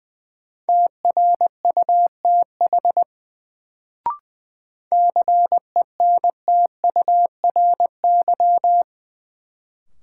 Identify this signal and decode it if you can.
Morse code, heard as a keyed tone, about 20 wpm, 700 Hz. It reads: TRUTH CENTURY
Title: Sets of 5 Words - New Word Drill - 20wpm